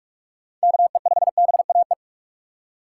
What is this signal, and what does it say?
Morse code, heard as a keyed tone, about 41 wpm, 700 Hz. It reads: KE5BAE
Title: USA Call Signs - 3x Character Spacing - 45wpm